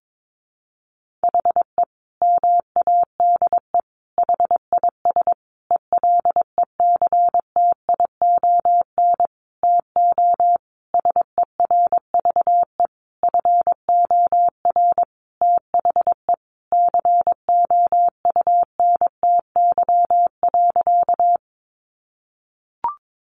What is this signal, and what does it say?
Morse code, heard as a keyed tone, about 22 wpm, 700 Hz. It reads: HE MADE HIS ELECTION TO SERVE FOR THE COUNTY.
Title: Sentences from Top 1500 Words 22wpm